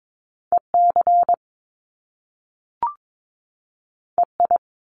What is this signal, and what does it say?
Morse code, heard as a keyed tone, about 22 wpm, 700 Hz. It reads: EC EI
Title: Top Two Letter Combinations in Words 22wpm